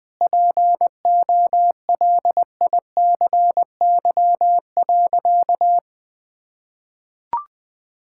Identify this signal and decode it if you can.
Morse code, heard as a keyed tone, about 20 wpm, 700 Hz. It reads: POLICY.
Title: Difficult Sentences with Hints - 20wpm